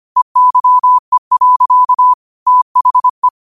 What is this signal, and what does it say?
Morse code, heard as a keyed tone, about 25 wpm, 1000 Hz. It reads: EYE. THE
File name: qotdfiles / 25wpm